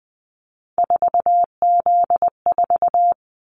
Morse code, heard as a keyed tone, about 20 wpm, 700 Hz. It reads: 4Z4
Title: Callsign Prefixes for Top 50 DXCC - 20wpm